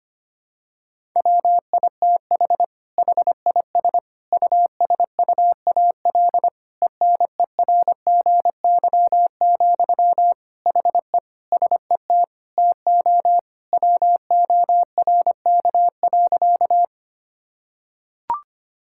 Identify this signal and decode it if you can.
Morse code, heard as a keyed tone, about 25 wpm, 700 Hz. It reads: WITH HIS USUAL ENERGY, HE SET TO WORK.
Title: Sentences from Top 1000 Words - Review Part 6 - 25wpm